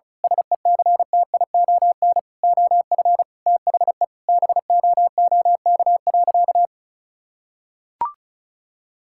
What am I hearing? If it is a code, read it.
Morse code, heard as a keyed tone, about 34 wpm, 700 Hz. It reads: SECTION OF THE BOOK.
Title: Sentences from Top 1000 Words - Review Part 5 - 35wpm